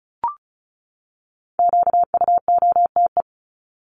Morse code, heard as a keyed tone, about 34 wpm, 700 Hz. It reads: QUOTE
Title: Top 1751-2000 Words 35wpm